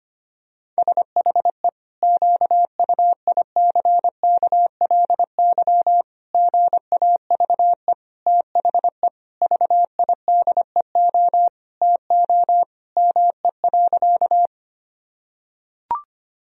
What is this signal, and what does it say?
Morse code, heard as a keyed tone, about 25 wpm, 700 Hz. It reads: SHE QUICKLY GAVE THE VIDEO TO ME.
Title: Sentences from Top 1000 Words - Review Part 4 - 25wpm